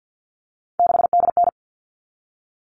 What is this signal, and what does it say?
Morse code, heard as a keyed tone, about 50 wpm, 700 Hz. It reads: 6DN
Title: All Cut Numbers - 3-Digit 50wpm